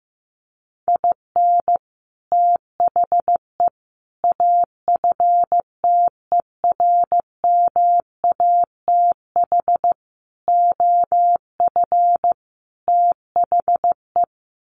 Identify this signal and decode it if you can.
Morse code, heard as a keyed tone, about 15 wpm, 700 Hz. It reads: IN THE AFTERMATH OF THE
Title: Sets of 5 Words 15wpm